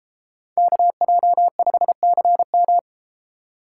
Morse code, heard as a keyed tone, about 32 wpm, 700 Hz. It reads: KJ5CM